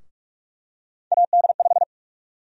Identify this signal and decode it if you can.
Morse code, heard as a keyed tone, about 47 wpm, 700 Hz. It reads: AD5
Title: USA Call Sign Prefixes - 45wpm